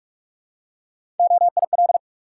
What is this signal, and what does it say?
Morse code, heard as a keyed tone, about 44 wpm, 700 Hz. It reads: OIL
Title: Three Letter Words - 45wpm